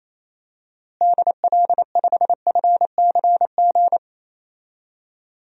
Morse code, heard as a keyed tone, about 28 wpm, 700 Hz. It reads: DL5FCG
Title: DXCC-100 Call Signs - 6 Characters - Repeated 2x - 28wpm